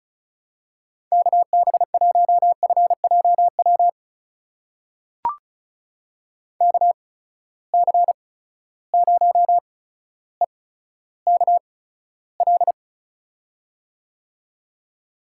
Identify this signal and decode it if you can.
Morse code, heard as a keyed tone, about 34 wpm, spaced as for 15 wpm, 700 Hz. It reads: KB1FJW KC0EKL
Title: USA Call Signs - 6 Characters - 8x Character Spacing - 35wpm